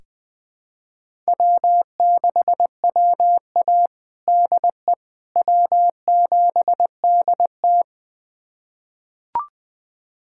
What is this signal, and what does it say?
Morse code, heard as a keyed tone, about 20 wpm, 700 Hz. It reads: W6WA DE W7DT
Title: USA Callsign DE Callsign - 4 Characters - 20wpm